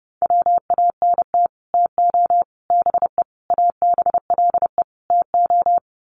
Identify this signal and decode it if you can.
Morse code, heard as a keyed tone, about 28 wpm, 700 Hz. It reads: WANT TO BE ABLE TO
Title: Sets of 5 Words 30wpm